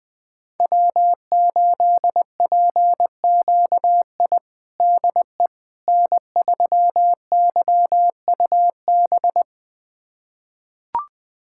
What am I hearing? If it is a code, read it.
Morse code, heard as a keyed tone, about 21 wpm, 700 Hz. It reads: W8PQI DE N3YUB
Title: USA Callsign DE Callsign - 5 Characters - 20wpm